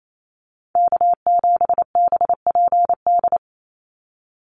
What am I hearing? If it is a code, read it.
Morse code, heard as a keyed tone, about 28 wpm, 700 Hz. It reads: K7BPD
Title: USA Call Signs 28wpm